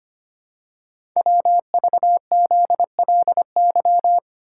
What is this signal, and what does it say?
Morse code, heard as a keyed tone, about 25 wpm, 700 Hz. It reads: WVZLY
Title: Code Groups - 5 Letters - 25wpm